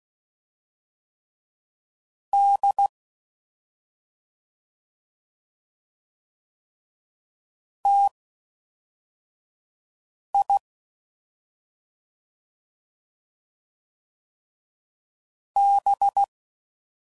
Morse code, heard as a keyed tone, about 16 wpm, spaced as for 4 wpm, 800 Hz. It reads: D TI B
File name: Mixed / CW Comparisons / CW Comparisons Broken Characters